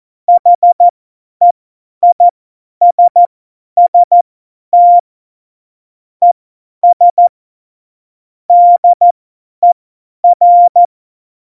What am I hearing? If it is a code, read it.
Morse code, heard as a keyed tone, about 14 wpm, spaced as for 10 wpm, 700 Hz. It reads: HEISST ES DER
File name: Morsekurs / Witze / Witz 21. Januar 2020, 50 F70